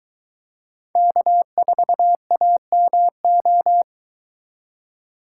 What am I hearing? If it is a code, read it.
Morse code, heard as a keyed tone, about 22 wpm, 700 Hz. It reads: K4AMO